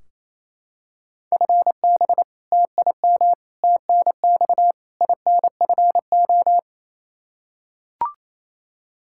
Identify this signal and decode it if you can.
Morse code, heard as a keyed tone, about 28 wpm, 700 Hz. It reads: FB TIM TNX INFO
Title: K1USN SST QSO Practice 28wpm